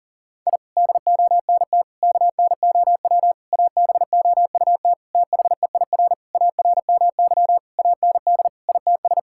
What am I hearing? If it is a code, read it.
Morse code, heard as a keyed tone, about 40 wpm, 700 Hz. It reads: I DONT KNOW ABOUT THEIR ARMY AND ITS
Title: Sentences from Top 1000 Words - Review Part 5 - 40wpm